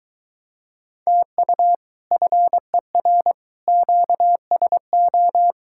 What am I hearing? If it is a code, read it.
Morse code, heard as a keyed tone, about 24 wpm, 700 Hz. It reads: TU FER QSO